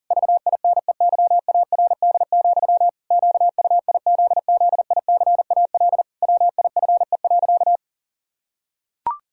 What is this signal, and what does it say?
Morse code, heard as a keyed tone, about 40 wpm, 700 Hz. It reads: VINEYARD, QUIZZICAL WIFE.
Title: Pangram Challenge v2 - 40wpm